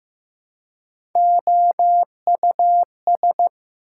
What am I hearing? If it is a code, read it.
Morse code, heard as a keyed tone, about 15 wpm, 700 Hz. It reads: OUS
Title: Prefixes and Suffixes 15wpm